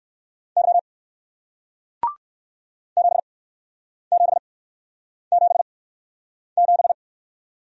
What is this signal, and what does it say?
Morse code, heard as a keyed tone, about 54 wpm, spaced as for 9 wpm, 700 Hz. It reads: Y 7777
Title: Speed-Racing Single Letter-Number 45wpm